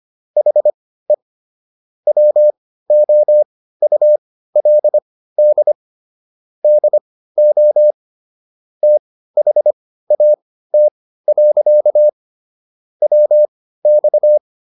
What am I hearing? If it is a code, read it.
Morse code, heard as a keyed tone, about 25 wpm, spaced as for 15 wpm, 600 Hz. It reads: HE WOULD DO THAT. WX